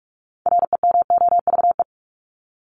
Morse code, heard as a keyed tone, about 44 wpm, 700 Hz. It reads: REMOVE
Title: ED Suffix - Top 100 Words with Base Words 45wpm